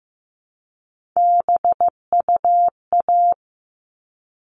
Morse code, heard as a keyed tone, about 15 wpm, 700 Hz. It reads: BUA
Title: All Cut Numbers - 3-Digit 15wpm